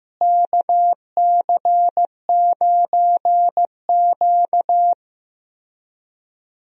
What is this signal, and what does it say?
Morse code, heard as a keyed tone, about 15 wpm, 700 Hz. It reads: KC9Q
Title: USA Call Signs - 4 Characters - Repeated 3x - 15wpm